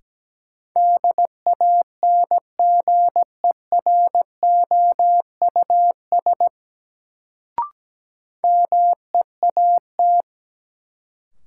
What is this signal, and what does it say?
Morse code, heard as a keyed tone, about 17 wpm, 700 Hz. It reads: DANGEROUS MEAT
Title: Sentences from Top 1200 Words - New Word Drill Alt - 17wpm